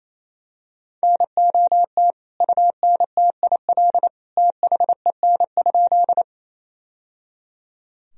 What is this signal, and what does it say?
Morse code, heard as a keyed tone, about 28 wpm, 700 Hz. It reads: NOT UNTIL THEN?
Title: Sentences from Top 1000 Words - Review Part 1 - 28wpm